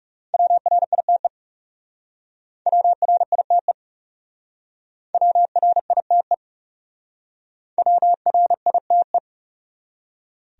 Morse code, heard as a keyed tone, about 40 wpm, 700 Hz. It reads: WRITE WRITE WRITE WRITE